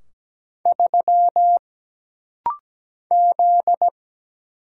Morse code, heard as a keyed tone, about 17 wpm, 700 Hz. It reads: 3 Z
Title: Single Letter-Number Alt - Rapid-Fire 17wpm